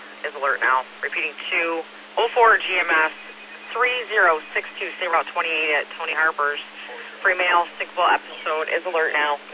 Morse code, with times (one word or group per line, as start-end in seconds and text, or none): none